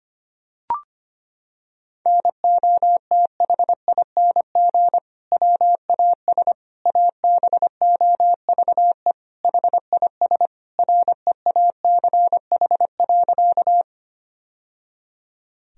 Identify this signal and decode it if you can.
Morse code, heard as a keyed tone, about 25 wpm, 700 Hz. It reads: NOTHING WAS ABOVE HIS REACH.
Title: Sentences from Top 1000 Words - Review Part 2 - 25wpm